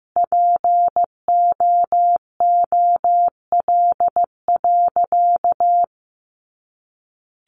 Morse code, heard as a keyed tone, about 15 wpm, 700 Hz. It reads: POOL.
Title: Easy Phrases 15wpm